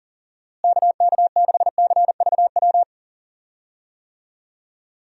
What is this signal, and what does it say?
Morse code, heard as a keyed tone, about 39 wpm, 700 Hz. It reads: KK6CVW
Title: USA Call Signs - 6 Characters - Repeated 3x - 40wpm